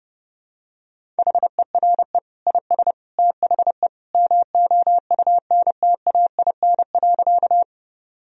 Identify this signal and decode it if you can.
Morse code, heard as a keyed tone, about 29 wpm, 700 Hz. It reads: HERE IS THE MOUNTAIN.
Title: Sentences from Top 1000 Words - Review Part 1 - 30wpm